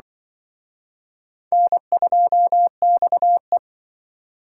Morse code, heard as a keyed tone, about 25 wpm, 700 Hz. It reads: N2XE
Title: Speed Racing - USA 1x2 Call Signs 20wpm